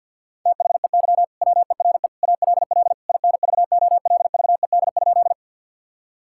Morse code, heard as a keyed tone, about 51 wpm, 700 Hz. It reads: THEY WERE ALL INVOLVED?
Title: Sentences from Top 1000 Words - Review Part 5 - 50wpm